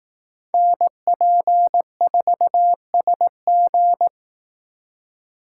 Morse code, heard as a keyed tone, about 18 wpm, 700 Hz. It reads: NP4SG
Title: Speed Racing - USA 2x2 Call Signs 15wpm